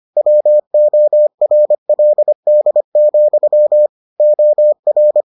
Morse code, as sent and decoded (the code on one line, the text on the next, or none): .-- --- .-. .-.. -.. --..-- / --- .-.
WORLD, OR